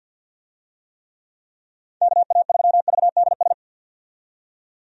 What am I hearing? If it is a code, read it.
Morse code, heard as a keyed tone, about 46 wpm, 700 Hz. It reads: KA3VDS
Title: USA Call Signs - 6 Characters - 50wpm